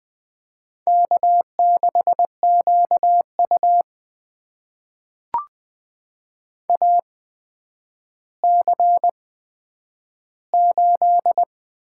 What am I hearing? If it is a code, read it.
Morse code, heard as a keyed tone, about 20 wpm, spaced as for 10 wpm, 700 Hz. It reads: K6QU AC8
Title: USA Call Signs - 8x Character Spacing - 20wpm